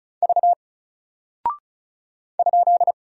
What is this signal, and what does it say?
Morse code, heard as a keyed tone, about 36 wpm, 700 Hz. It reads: V ?
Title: Single Character (Letter:Number:Punctuation:Prosign) Alt 35wpm